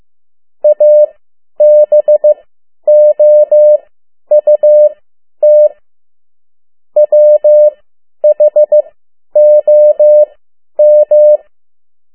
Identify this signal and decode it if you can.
Morse code, heard as a keyed tone, about 15 wpm, spaced as for 10 wpm, 600 Hz. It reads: ABOUT WHOM
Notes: Morse code 10 WPM